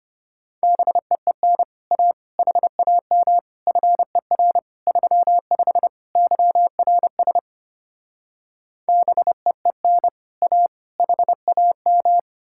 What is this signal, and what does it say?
Morse code, heard as a keyed tone, about 33 wpm, 700 Hz. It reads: BEEN A HAM FER 35 YRS BEEN A HAM